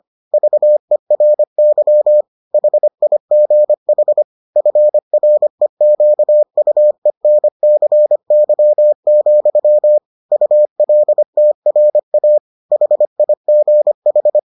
Morse code, heard as a keyed tone, about 25 wpm, 600 Hz. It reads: VERY HIGH FREQUENCY, ULTRA HIGH